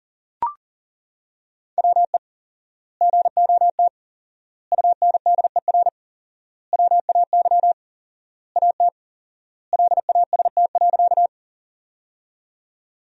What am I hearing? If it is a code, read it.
Morse code, heard as a keyed tone, about 40 wpm, 700 Hz. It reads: WE GOT UNDER WAY AT LAST.
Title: 4x Word-Spacing - Sentences from Top 300 Words - 40wpm